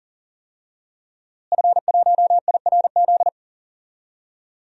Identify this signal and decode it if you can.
Morse code, heard as a keyed tone, about 40 wpm, 700 Hz. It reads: F1IRZ
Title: DXCC-100 Call Signs - 5 Characters - Repeated 3x - 40wpm